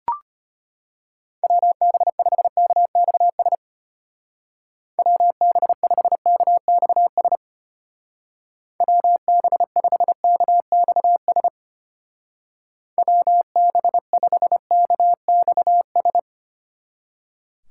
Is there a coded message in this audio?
Yes: Morse code from 1.4 to 16.2 s, heard as a keyed tone, about 32 wpm, 700 Hz. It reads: WB5KXS WB5KXS WB5KXS WB5KXS